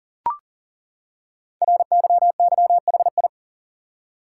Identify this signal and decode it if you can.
Morse code, heard as a keyed tone, about 41 wpm, 700 Hz. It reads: RYYHI
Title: Code Groups - 5 Letters - 40wpm